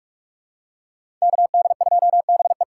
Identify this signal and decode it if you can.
Morse code, heard as a keyed tone, about 46 wpm, 700 Hz. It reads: KD2BE
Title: USA Call Signs - 6x Character Spacing - 45wpm